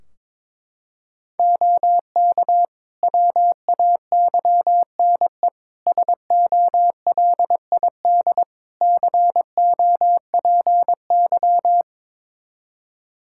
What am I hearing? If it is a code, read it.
Morse code, heard as a keyed tone, about 22 wpm, 700 Hz. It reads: OK WAYNE SOLID COPY
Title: Practice QSO - Exchange 3 - More Info - 22wpm